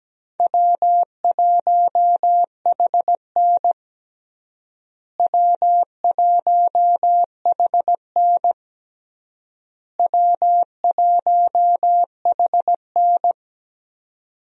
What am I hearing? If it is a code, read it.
Morse code, heard as a keyed tone, about 17 wpm, 700 Hz. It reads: W1HN W1HN W1HN
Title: USA Call Signs - 4 Characters - Repeated 3x - 17wpm